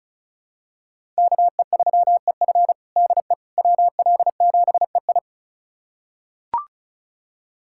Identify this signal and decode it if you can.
Morse code, heard as a keyed tone, about 36 wpm, 700 Hz. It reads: KE3EF DE WL7EI
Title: USA Callsign DE Callsign - 5 Characters - 35wpm